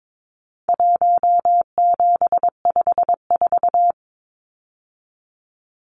Morse code, heard as a keyed tone, about 22 wpm, 700 Hz. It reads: 1754
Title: Numbers - Contest Style - 4-Digit 22wpm